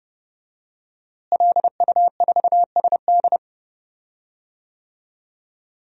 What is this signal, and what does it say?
Morse code, heard as a keyed tone, about 30 wpm, 700 Hz. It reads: LU4SD
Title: DXCC-100 Call Signs - 5 Characters - Repeated 2x - 30wpm